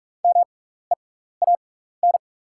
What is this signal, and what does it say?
Morse code, heard as a keyed tone, about 47 wpm, spaced as for 16 wpm, 700 Hz. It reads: MEAN
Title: 6x Character Spacing - Top 200 Words - 45wpm